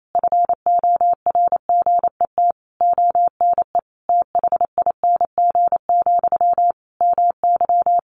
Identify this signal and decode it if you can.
Morse code, heard as a keyed tone, about 28 wpm, 700 Hz. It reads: FORGET ONE THING, MY
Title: Sentences from Top 1200 Words 28wpm